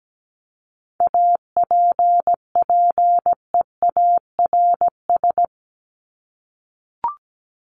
Morse code, heard as a keyed tone, about 17 wpm, 700 Hz. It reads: APPEARS
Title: ES:S Suffix - Top 100 Words with Base Words 17wpm